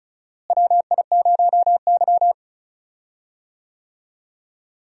Morse code, heard as a keyed tone, about 34 wpm, 700 Hz. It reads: WI0Y
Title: USA Call Signs - 4 Characters - Repeated 2x - 35wpm